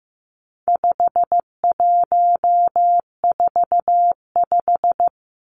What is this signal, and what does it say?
Morse code, heard as a keyed tone, about 15 wpm, 700 Hz. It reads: 5145
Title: Numbers - 4-Digit 15wpm